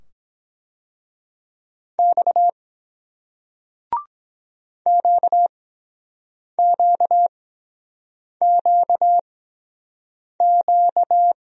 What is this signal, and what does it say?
Morse code, heard as a keyed tone, about 23 wpm, spaced as for 7 wpm, 700 Hz. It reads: X QQQQ